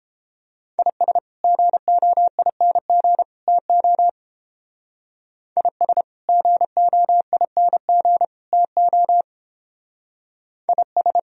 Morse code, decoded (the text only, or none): IS GOING TO IS GOING TO IS